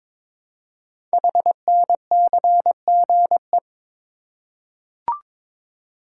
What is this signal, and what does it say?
Morse code, heard as a keyed tone, about 22 wpm, 700 Hz. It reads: HNCGE
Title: Code Groups - 5 Letters - 22wpm